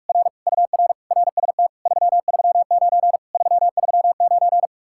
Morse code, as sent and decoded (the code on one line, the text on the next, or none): .-. / ..- .-. / .-. ... - / ...-- ...-- ----. / ...-- ...-- ----.
R UR RST 339 339